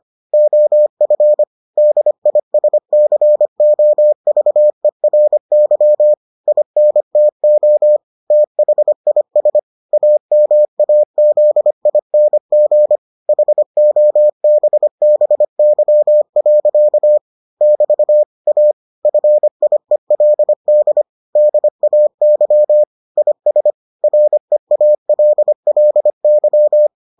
Morse code, heard as a keyed tone, about 25 wpm, 600 Hz. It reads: OF DISCOVERY INTO THIS AMAZING HOBBY. <BT> A FIELD DAY IS REALLY